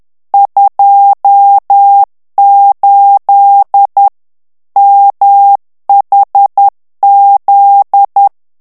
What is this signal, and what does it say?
Morse code, heard as a keyed tone, about 10 wpm, 800 Hz. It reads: 28 MHZ
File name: ANGLAIS texte clair 3 vitesse 10 mots minute